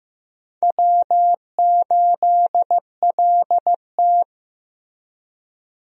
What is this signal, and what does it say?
Morse code, heard as a keyed tone, about 15 wpm, 700 Hz. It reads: W8LT
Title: USA Call Signs - 4 Characters - Repeated 3x - 15wpm